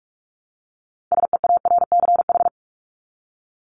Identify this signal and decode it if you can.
Morse code, heard as a keyed tone, about 44 wpm, 700 Hz. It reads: SEARCH
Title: Top 501-750 Words 45wpm